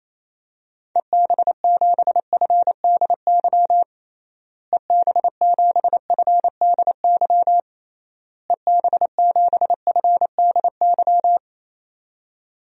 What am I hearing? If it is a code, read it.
Morse code, heard as a keyed tone, about 27 wpm, 700 Hz. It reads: EB7FDY EB7FDY EB7FDY
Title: DXCC-100 Call Signs - 6 Characters - Repeated 3x - 28wpm